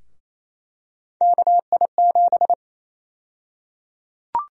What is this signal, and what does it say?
Morse code, heard as a keyed tone, about 28 wpm, 700 Hz. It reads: KI7
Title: USA Call Sign Prefixes - Repeated 2x - 28wpm